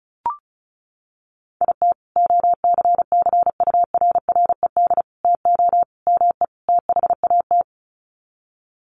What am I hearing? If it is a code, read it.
Morse code, heard as a keyed tone, about 34 wpm, 700 Hz. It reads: IT OCCURRED TO ME THAT
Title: Sets of 5 Words Encore 35wpm